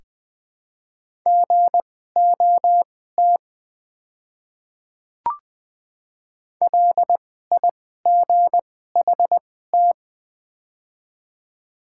Morse code, heard as a keyed tone, about 20 wpm, spaced as for 15 wpm, 700 Hz. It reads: GOT LIGHT
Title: 2x Character Spacing - Top 300 Words - 20wpm